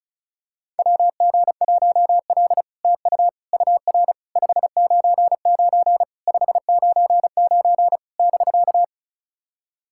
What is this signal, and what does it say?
Morse code, heard as a keyed tone, about 36 wpm, 700 Hz. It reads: WG1L TU UR 599 599 <BK>
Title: SOTA QSO Practice 35wpm